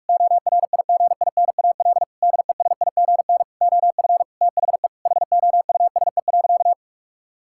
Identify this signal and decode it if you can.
Morse code, heard as a keyed tone, about 46 wpm, 700 Hz. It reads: ORIGINAL DESIGN OF THE HOUSE.
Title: Sentences from Top 1000 Words - Review Part 6 - 45wpm